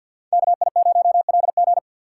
Morse code, heard as a keyed tone, about 63 wpm, 700 Hz. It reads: KI0LG